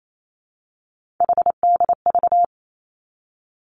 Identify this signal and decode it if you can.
Morse code, heard as a keyed tone, about 28 wpm, 700 Hz. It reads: HDV
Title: Three Challenging Letter-Number Combinations 28wpm